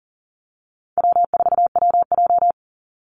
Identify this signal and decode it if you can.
Morse code, heard as a keyed tone, about 40 wpm, 700 Hz. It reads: W4WJ
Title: Easy Call Signs - 40wpm